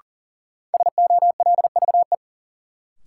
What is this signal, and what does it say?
Morse code, heard as a keyed tone, about 40 wpm, 700 Hz. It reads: SOLVE
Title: Sentences from Top 1000 Words - New Word Drill - 40wpm